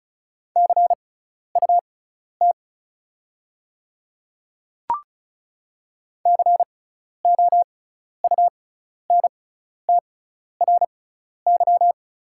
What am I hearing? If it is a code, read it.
Morse code, heard as a keyed tone, about 34 wpm, spaced as for 12 wpm, 700 Hz. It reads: CUT COUNTRY
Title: 6x Character Spacing - Top 300 Words - 35wpm